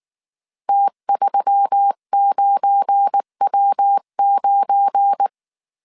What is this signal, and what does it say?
Morse code, heard as a keyed tone, about 19 wpm, 800 Hz. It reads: T39W9
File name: CW / DET / 1 9 1500C S B 18wpm 2551